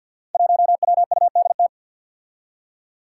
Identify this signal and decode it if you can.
Morse code, heard as a keyed tone, about 51 wpm, 700 Hz. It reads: 1WUDT
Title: Five Challenging Letter-Number Combinations 50wpm